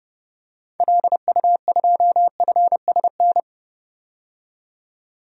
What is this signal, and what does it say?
Morse code, heard as a keyed tone, about 30 wpm, 700 Hz. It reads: LU2FSN
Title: DXCC-100 Call Signs - 6 Characters - 30wpm